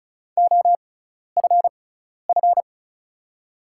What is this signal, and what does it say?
Morse code, heard as a keyed tone, about 36 wpm, spaced as for 12 wpm, 700 Hz. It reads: OFF
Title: 6x Character Spacing - Top 200 Words - 35wpm